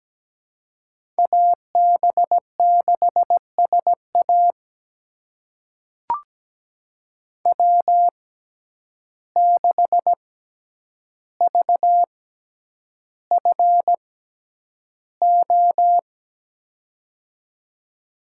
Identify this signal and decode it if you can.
Morse code, heard as a keyed tone, about 17 wpm, spaced as for 9 wpm, 700 Hz. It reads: AB6SA W6VFO
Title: USA Call Signs - 6x Character Spacing - 17wpm